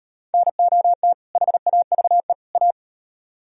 Morse code, heard as a keyed tone, about 38 wpm, 700 Hz. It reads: NOT HAVE A